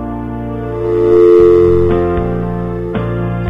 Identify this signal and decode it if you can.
Dial tone (North America), 350 + 440 Hz.